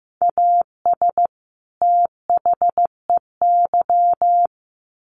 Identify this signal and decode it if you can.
Morse code, heard as a keyed tone, about 15 wpm, 700 Hz. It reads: AS THEY
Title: Sets of 2 Words 15wpm